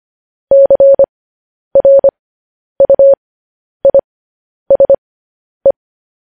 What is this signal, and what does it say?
Morse code, heard as a keyed tone, about 25 wpm, spaced as for 10 wpm, 550 Hz. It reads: CRUISE
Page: News Headlines in Morse Code at 10 WPM